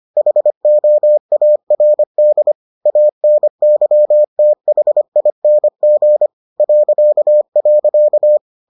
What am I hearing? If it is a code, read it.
Morse code, heard as a keyed tone, about 25 wpm, 600 Hz. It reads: HOARD ANYTHING ..